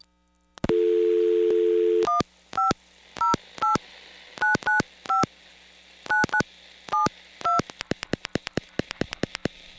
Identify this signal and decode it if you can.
DTMF, heard as telephone touch tones, tones of about 140 ms. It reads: 1508995 9972
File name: blackratchet / recordings / charles-st-payphone